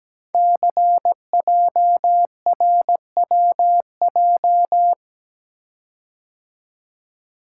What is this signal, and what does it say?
Morse code, heard as a keyed tone, about 17 wpm, 700 Hz. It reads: CJRWJ
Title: Code Groups - 5 Letters - 17wpm